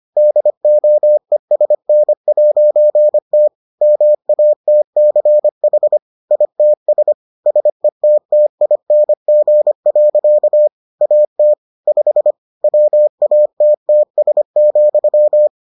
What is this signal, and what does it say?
Morse code, heard as a keyed tone, about 25 wpm, 600 Hz. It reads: DOESN'T MATCH ITS SETTING. AT 5 WATTS,